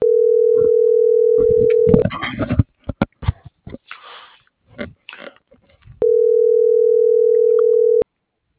Ringback tone (North America), 440 + 480 Hz.